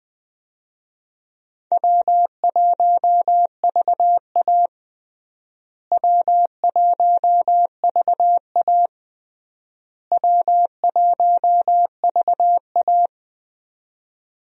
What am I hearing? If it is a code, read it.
Morse code, heard as a keyed tone, about 20 wpm, 700 Hz. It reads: W1VA W1VA W1VA
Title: USA Call Signs - 4 Characters - Repeated 3x - 20wpm